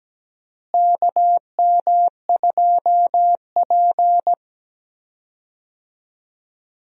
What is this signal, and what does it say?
Morse code, heard as a keyed tone, about 17 wpm, 700 Hz. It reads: KM2P
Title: USA Call Signs - 4 Characters - Repeated 3x - 17wpm